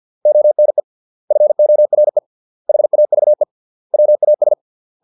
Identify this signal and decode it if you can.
Morse code, heard as a keyed tone, about 50 wpm, 600 Hz. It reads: ONE FORE HAVE WAS